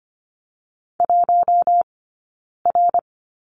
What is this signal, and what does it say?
Morse code, heard as a keyed tone, about 25 wpm, 700 Hz. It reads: 1 R
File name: Lesson 15d - Number 5 - Two Characters 25wpm